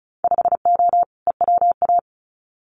Morse code, heard as a keyed tone, about 34 wpm, 700 Hz. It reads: 5O EWA
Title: Winter Field Day Exchanges Encore 35wpm